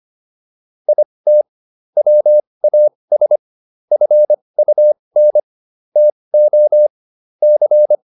Morse code, heard as a keyed tone, about 25 wpm, spaced as for 20 wpm, 600 Hz. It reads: IT WAS FUN TO C